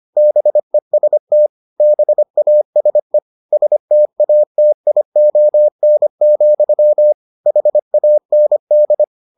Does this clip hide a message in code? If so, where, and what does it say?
Yes: Morse code from 0.2 to 9.0 s, heard as a keyed tone, about 25 wpm, 600 Hz. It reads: BEST BASE STATION, HAND